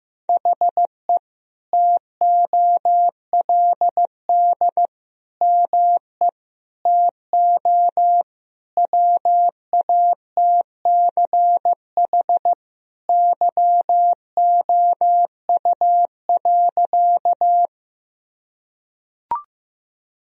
Morse code, heard as a keyed tone, about 15 wpm, 700 Hz. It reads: HE TOLD ME TO WATCH YOU.
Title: Sentences from Top 1000 Words - Review Part 1 - 15wpm